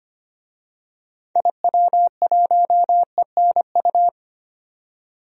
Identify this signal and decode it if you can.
Morse code, heard as a keyed tone, about 25 wpm, 700 Hz. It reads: IW1ENU
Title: DXCC-100 Call Signs - 6 Characters - 25wpm